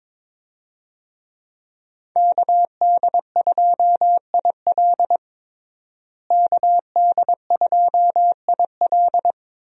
Morse code, heard as a keyed tone, about 22 wpm, 700 Hz. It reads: KD2IL KD2IL
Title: USA Call Signs - 5 Characters - Repeated 2x - 22wpm